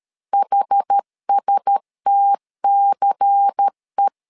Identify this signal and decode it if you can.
Morse code, heard as a keyed tone, about 13 wpm, 800 Hz. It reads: HSTCE
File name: LEZIONI / 5 LBVC / 5 LBVC 12wpm 1955